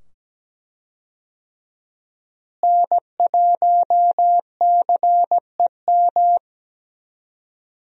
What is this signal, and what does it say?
Morse code, heard as a keyed tone, about 17 wpm, 700 Hz. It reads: N1CEM
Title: USA Call Signs - 5 Characters - Repeated 2x - 17wpm